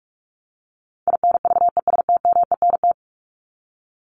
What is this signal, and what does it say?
Morse code, heard as a keyed tone, about 44 wpm, 700 Hz. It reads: INVESTMENT
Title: Top 501-750 Words 45wpm